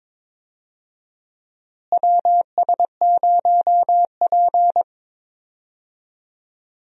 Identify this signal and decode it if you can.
Morse code, heard as a keyed tone, about 21 wpm, 700 Hz. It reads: WS0P
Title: USA Call Signs - 4 Characters - Repeated 3x - 22wpm